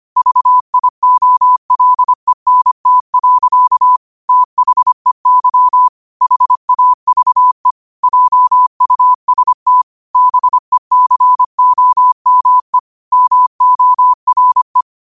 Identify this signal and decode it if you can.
Morse code, heard as a keyed tone, about 25 wpm, 1000 Hz. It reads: VIOLENT. THEY HAVE JUST BECOME MORE
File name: 25wpm